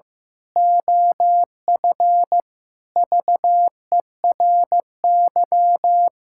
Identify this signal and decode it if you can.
Morse code, heard as a keyed tone, about 16 wpm, 700 Hz. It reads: OF VERY
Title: Pangram Challenge v2 - 15wpm